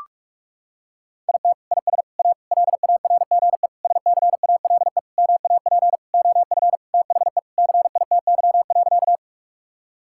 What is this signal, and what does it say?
Morse code, heard as a keyed tone, about 41 wpm, 700 Hz. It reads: IT IS A LARGE SCALE MAP OF THE CITY.
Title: Sentences from Top 1000 Words - Review Part 5 - 45wpm